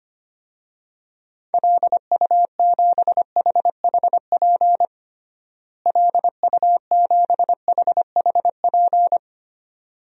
Morse code, heard as a keyed tone, about 25 wpm, 700 Hz. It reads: LU7HHP LU7HHP
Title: DXCC-100 Call Signs - 6 Characters - Repeated 2x - 25wpm